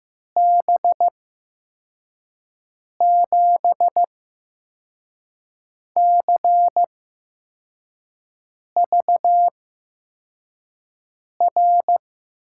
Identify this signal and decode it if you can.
Morse code, heard as a keyed tone, about 16 wpm, spaced as for 4 wpm, 700 Hz. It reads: B7CVR
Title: USA Call Signs - 6 Characters - 8x Character Spacing - 15wpm